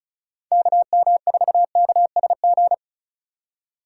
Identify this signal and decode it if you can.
Morse code, heard as a keyed tone, about 34 wpm, 700 Hz. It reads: KM4KSG
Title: USA Call Signs - 6 Characters - 2x Character Spacing - 35wpm